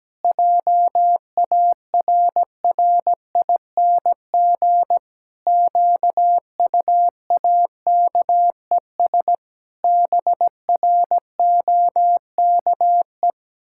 Morse code, heard as a keyed tone, about 17 wpm, 700 Hz. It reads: JARRING QUAKES BROKE
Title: Pangram Challenge v2 - 17wpm